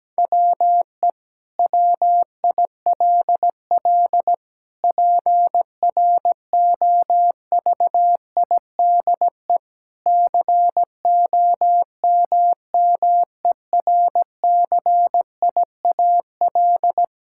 Morse code, heard as a keyed tone, about 17 wpm, 700 Hz. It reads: WE WILL PROVIDE COMMERCIAL
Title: Sentences from Top 1000 Words - Review Part 5 - 17wpm